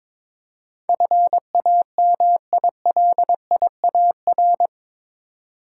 Morse code, heard as a keyed tone, about 22 wpm, 700 Hz. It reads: FAMILIAR
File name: practice / 22wpm / Eight Letter Words - 22wpm